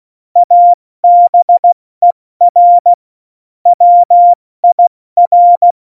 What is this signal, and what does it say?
Morse code, heard as a keyed tone, about 16 wpm, 700 Hz. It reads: ABER WIR
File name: Morsekurs / Witze / Witz 28. Januar 2020, 70 F80